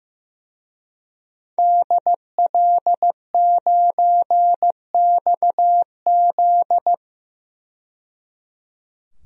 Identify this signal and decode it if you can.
Morse code, heard as a keyed tone, about 15 wpm, 700 Hz. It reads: DL9XZ
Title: DXCC-100 Call Signs - 5 Characters - 15wpm